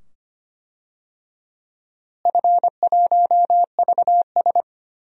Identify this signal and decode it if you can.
Morse code, heard as a keyed tone, about 25 wpm, 700 Hz. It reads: F1VS
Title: DXCC-100 Call Signs - 4 Characters - Repeated 2x - 25wpm